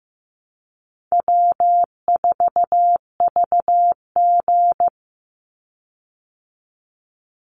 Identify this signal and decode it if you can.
Morse code, heard as a keyed tone, about 15 wpm, 700 Hz. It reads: W4VG
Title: Easy Call Signs Repeated 3x - 15wpm